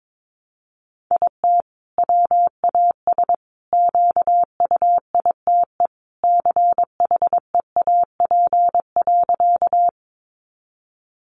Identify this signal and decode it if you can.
Morse code, heard as a keyed tone, about 22 wpm, 700 Hz. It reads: IT WAS QUITE CHEAP.
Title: Sentences from Top 1200 Words 22wpm